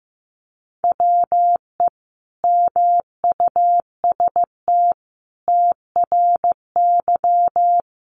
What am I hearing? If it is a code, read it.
Morse code, heard as a keyed tone, about 15 wpm, 700 Hz. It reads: WE MUST TRY
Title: Sentences from Top 1200 Words 15wpm